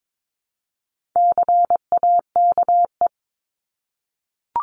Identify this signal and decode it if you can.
Morse code, heard as a keyed tone, about 22 wpm, 700 Hz. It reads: CAKE
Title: Top 1501-1750 Words 22wpm